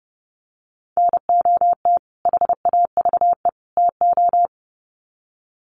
Morse code, heard as a keyed tone, about 30 wpm, 700 Hz. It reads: NOT HAVE TO
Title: Sets of 3 Words 30wpm